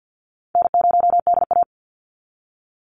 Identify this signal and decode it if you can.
Morse code, heard as a keyed tone, about 51 wpm, 700 Hz. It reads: N0DA
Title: NA SOTA Chaser Callsigns 50wpm